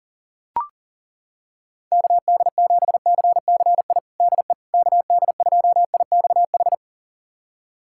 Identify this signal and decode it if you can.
Morse code, heard as a keyed tone, about 40 wpm, 700 Hz. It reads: KD7CCI DE KD2IXH
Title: USA Callsign DE Callsign - 6 Characters - 40wpm